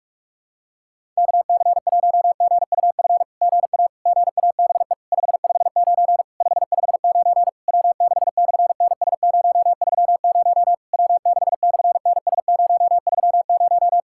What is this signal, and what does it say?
Morse code, heard as a keyed tone, about 42 wpm, 690 Hz. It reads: KC1GUF GA GABE 559 559 W6/NS030 W6/NS030
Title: SOTA QSO Practice 45wpm